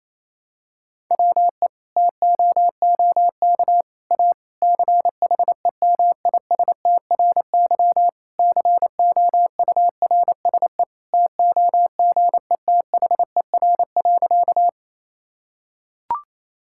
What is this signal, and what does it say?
Morse code, heard as a keyed tone, about 28 wpm, 700 Hz. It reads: WE TOOK A CHEMISTRY COURSE TOGETHER.
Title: Sentences from Top 1000 Words - Review Part 6 - 28wpm